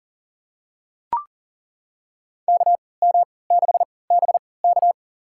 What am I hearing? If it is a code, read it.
Morse code, heard as a keyed tone, about 40 wpm, spaced as for 23 wpm, 690 Hz. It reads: KM6BK
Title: USA Call Signs - 3x Character Spacing - 40wpm